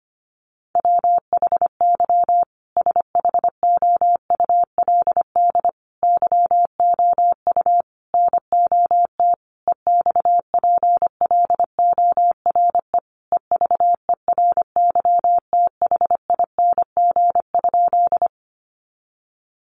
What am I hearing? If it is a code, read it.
Morse code, heard as a keyed tone, about 25 wpm, 700 Hz. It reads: WHY SHOULD YOU NOT EXPLORE EVERYTHING?
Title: Sentences from Top 1100 Words 25wpm